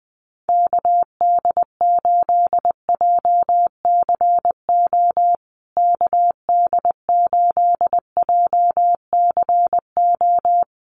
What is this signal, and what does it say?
Morse code, heard as a keyed tone, about 20 wpm, 700 Hz. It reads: KD8JCO KD8JCO
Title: USA Call Signs Repeated 3x 20wpm